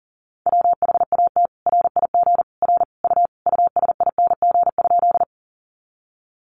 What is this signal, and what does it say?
Morse code, heard as a keyed tone, about 40 wpm, 700 Hz. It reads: WHAT RIG R U USING?
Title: Common QSO Phrases 40wpm